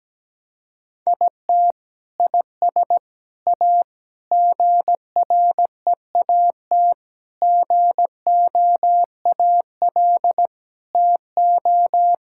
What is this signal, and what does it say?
Morse code, heard as a keyed tone, about 17 wpm, 700 Hz. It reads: IT IS A GREAT GOAL TO
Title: Sentences from Top 1000 Words - Review Part 5 - 17wpm